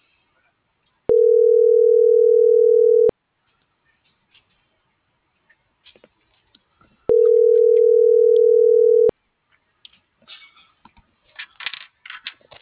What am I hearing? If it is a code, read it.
Ringback tone (North America), 440 + 480 Hz.